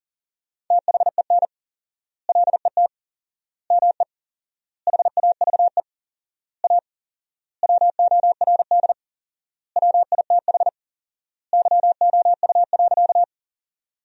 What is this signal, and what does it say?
Morse code, heard as a keyed tone, about 40 wpm, 700 Hz. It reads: THEN LET ME HAVE A WORD WITH YOU.
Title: 4x Word-Spacing - Sentences from Top 300 Words - 40wpm